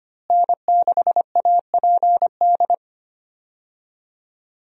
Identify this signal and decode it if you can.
Morse code, heard as a keyed tone, about 25 wpm, 700 Hz. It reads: N6APD
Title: USA Call Signs - 5 Characters - Repeated 2x - 25wpm